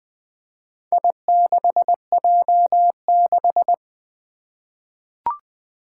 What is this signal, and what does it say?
Morse code, heard as a keyed tone, about 21 wpm, 700 Hz. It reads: I6J6
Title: Four Challenging Letter-Number Combinations 20wpm